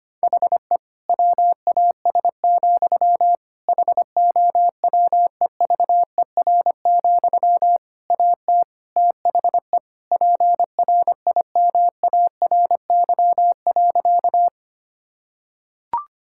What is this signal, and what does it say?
Morse code, heard as a keyed tone, about 25 wpm, 700 Hz. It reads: HE WAS, HOWEVER, AT THE PRIMARY.
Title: Sentences from Top 1000 Words - Review Part 5 - 25wpm